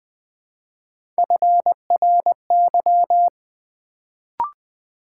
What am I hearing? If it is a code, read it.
Morse code, heard as a keyed tone, about 20 wpm, 700 Hz. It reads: FRY
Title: Build A Word - 1 Letter At A Time - 3-Letter Words - 20wpm